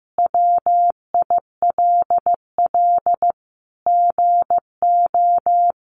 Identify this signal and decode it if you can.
Morse code, heard as a keyed tone, about 15 wpm, 700 Hz. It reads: WILL GO
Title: Sentences from Top 1300 Words 15wpm